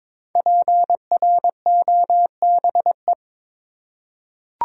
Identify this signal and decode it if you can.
Morse code, heard as a keyed tone, about 22 wpm, 700 Hz. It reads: PROBE
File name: 5 Letter Words - First 3 Letters Given - 22wpm